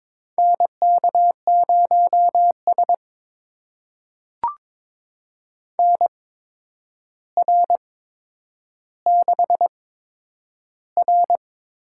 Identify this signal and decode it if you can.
Morse code, heard as a keyed tone, about 22 wpm, spaced as for 9 wpm, 700 Hz. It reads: NK0S NR6R
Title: USA Call Signs - 4 Characters - 8x Character Spacing - 22wpm